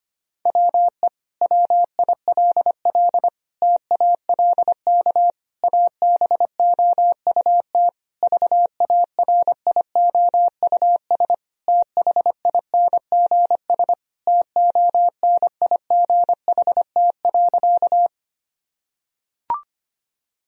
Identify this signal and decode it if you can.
Morse code, heard as a keyed tone, about 25 wpm, 700 Hz. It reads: WE WILL TALK ABOUT VARIOUS THINGS TONIGHT.
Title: Sentences from Top 1000 Words - Review Part 4 - 25wpm